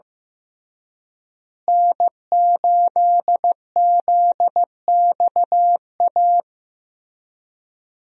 Morse code, heard as a keyed tone, about 16 wpm, 700 Hz. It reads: N8ZXA
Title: USA Call Signs - 5 Characters - Repeated 3x - 15wpm